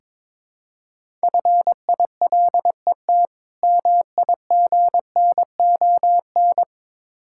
Morse code, heard as a keyed tone, about 22 wpm, 690 Hz. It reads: FILET MIGNON
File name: Difficult Words with Hints - 22wpm